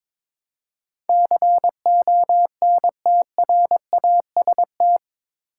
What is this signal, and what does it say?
Morse code, heard as a keyed tone, about 22 wpm, 700 Hz. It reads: CONTRAST
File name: practice / 22wpm / Eight Letter Words - 22wpm